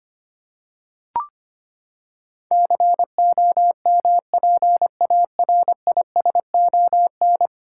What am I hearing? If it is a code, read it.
Morse code, heard as a keyed tone, about 25 wpm, 700 Hz. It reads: COMPARISON
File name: Ten Letter Words - 25wpm